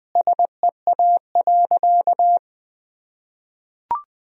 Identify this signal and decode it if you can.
Morse code, heard as a keyed tone, about 21 wpm, 700 Hz. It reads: SEA.
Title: Sentences from Top 1000 Words - Review Part 1 - 20wpm